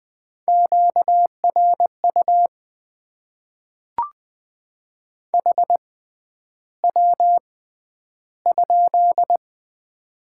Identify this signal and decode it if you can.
Morse code, heard as a keyed tone, about 20 wpm, spaced as for 10 wpm, 700 Hz. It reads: QRU HW?
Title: Top 90 QSO Elements - 6x Character Spacing - 20wpm